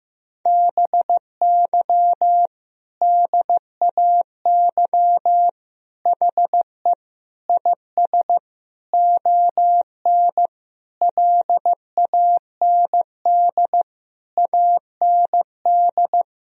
Morse code, heard as a keyed tone, about 15 wpm, 700 Hz. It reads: BY DAY HE IS ON LAND AND
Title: Sentences from Top 1000 Words - Review Part 1 - 15wpm